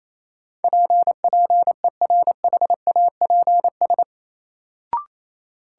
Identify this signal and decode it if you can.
Morse code, heard as a keyed tone, about 27 wpm, 700 Hz. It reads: P PERHAPS
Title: Sentences from Top 1100 Words - New Word Drill - 28wpm